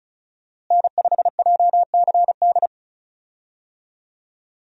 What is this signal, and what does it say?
Morse code, heard as a keyed tone, about 34 wpm, 700 Hz. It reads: N5JCD
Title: USA Call Signs - 5 Characters - Repeated 2x - 35wpm